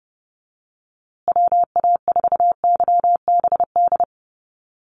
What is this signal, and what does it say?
Morse code, heard as a keyed tone, about 30 wpm, 700 Hz. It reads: WA4YBD
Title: USA Call Signs Repeated 3x 30wpm